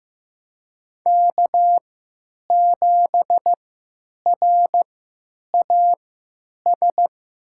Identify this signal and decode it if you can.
Morse code, heard as a keyed tone, about 15 wpm, spaced as for 9 wpm, 700 Hz. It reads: K7RAS
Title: USA Call Signs - 3x Character Spacing - 15wpm